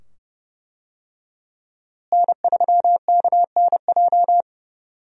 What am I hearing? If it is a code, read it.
Morse code, heard as a keyed tone, about 32 wpm, 700 Hz. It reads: N3KNJ
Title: USA Call Signs - 5 Characters - Repeated 2x - 30wpm